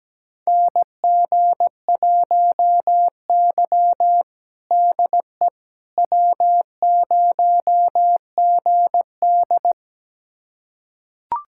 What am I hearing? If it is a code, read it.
Morse code, heard as a keyed tone, about 17 wpm, 700 Hz. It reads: NG1Y DE W0GD
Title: USA Callsign DE Callsign - 4 Characters - 17wpm